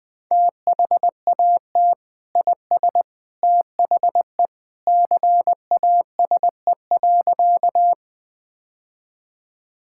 Morse code, heard as a keyed tone, about 20 wpm, 700 Hz. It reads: THAT IS THE CASE.
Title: Sentences with 2-Word Phrases - Spoken Beginning - 20wpm